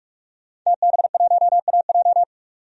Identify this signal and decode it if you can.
Morse code, heard as a keyed tone, about 46 wpm, 700 Hz. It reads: TB1AJ
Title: DXCC-100 Call Signs - 5 Characters - 45wpm